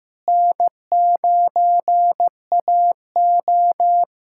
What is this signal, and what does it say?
Morse code, heard as a keyed tone, about 15 wpm, 700 Hz. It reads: N9AO
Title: USA Callsign DE Callsign - 4 Characters - 15wpm